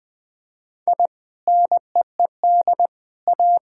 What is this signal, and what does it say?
Morse code, heard as a keyed tone, about 20 wpm, 700 Hz. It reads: I NEED A
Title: Sentences with 3-Word Phrases - 20wpm